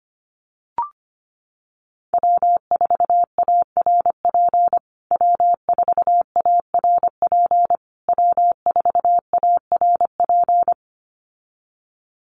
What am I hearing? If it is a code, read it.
Morse code, heard as a keyed tone, about 25 wpm, 700 Hz. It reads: W4ARP W4ARP W4ARP
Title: USA Call Signs Repeated 3x 25wpm